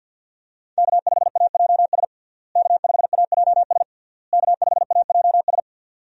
Morse code, heard as a keyed tone, about 52 wpm, 700 Hz. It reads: K5AJS K5AJS K5AJS
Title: USA Call Signs - 5 Characters - Repeated 3x - 50wpm